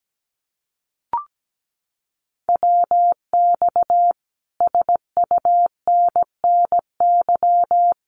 Morse code, heard as a keyed tone, about 17 wpm, 700 Hz. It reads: WX SUNNY
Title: Common QSO Phrases Encore 17wpm